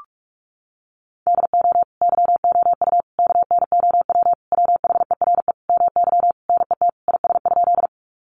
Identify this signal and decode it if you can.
Morse code, heard as a keyed tone, about 44 wpm, 700 Hz. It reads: DO YOU KNOW WHERE MY NET IS?
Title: Sentences from Top 1400 Words 45wpm